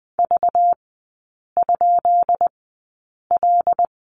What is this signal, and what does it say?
Morse code, heard as a keyed tone, about 20 wpm, spaced as for 8 wpm, 700 Hz. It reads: V?L
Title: 191 Lesson 31e - Letter v - Three Characters 20wpm